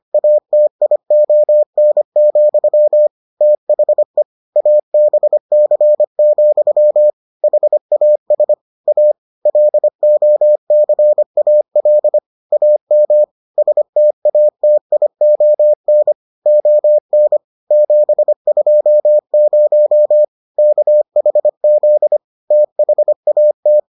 Morse code, heard as a keyed tone, about 25 wpm, 600 Hz. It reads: ATION, THE ABC, HAS A LOCAL AM STATION ON 720 KHZ THAT